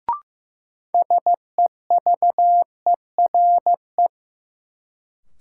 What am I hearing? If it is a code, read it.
Morse code, heard as a keyed tone, about 15 wpm, 700 Hz. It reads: SEVERE
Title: Sentences from Top 1400 Words - New Word Drill - 15wpm